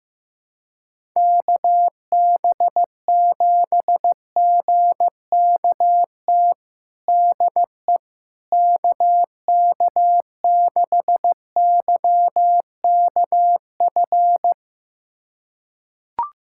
Morse code, heard as a keyed tone, about 16 wpm, 700 Hz. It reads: KB7GKT DE KK6YKF
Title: USA Callsign DE Callsign 15wpm